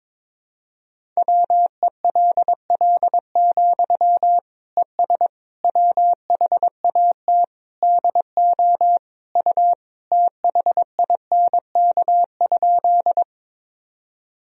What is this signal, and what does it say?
Morse code, heard as a keyed tone, about 22 wpm, 700 Hz. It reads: WELL, ES WHAT DO U THINK?
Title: Sentences from Top 200 Words and Abbreviations 22wpm